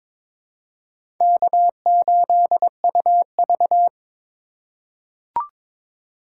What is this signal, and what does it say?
Morse code, heard as a keyed tone, about 22 wpm, 700 Hz. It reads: K8UV
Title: USA Call Signs - 4 Characters - 6x Character Spacing - 22wpm